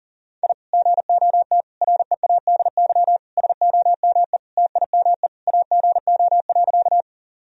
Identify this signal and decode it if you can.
Morse code, heard as a keyed tone, about 37 wpm, 700 Hz. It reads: I GOT READY SOME TIME AGO.
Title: Sentences from Top 1000 Words - Review Part 3 - 40wpm